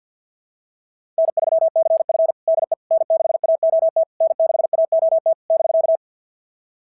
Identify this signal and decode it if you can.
Morse code, heard as a keyed tone, about 50 wpm, 650 Hz. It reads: N3CF DE N6AOT N6AOT <BK>
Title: SOTA QSO Practice 50wpm